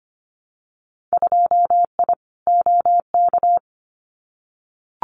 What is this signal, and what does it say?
Morse code, heard as a keyed tone, about 25 wpm, 700 Hz. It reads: 2I OK
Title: Winter Field Day Exchanges Encore 25wpm